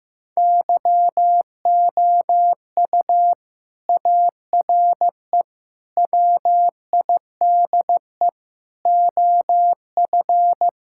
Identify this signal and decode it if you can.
Morse code, heard as a keyed tone, about 16 wpm, 700 Hz. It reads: YOU ARE WIDE OF
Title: Sentences from Top 1000 Words - Review Part 6 - 15wpm